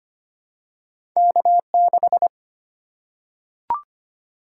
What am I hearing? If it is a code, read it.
Morse code, heard as a keyed tone, about 25 wpm, 700 Hz. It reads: K6
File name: Two Letter-Number Combinations - 25wpm